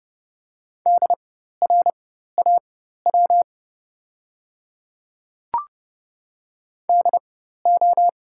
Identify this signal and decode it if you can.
Morse code, heard as a keyed tone, about 30 wpm, spaced as for 14 wpm, 700 Hz. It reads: DRAW DO
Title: 4x Character Spacing - Top 300 Words - 30wpm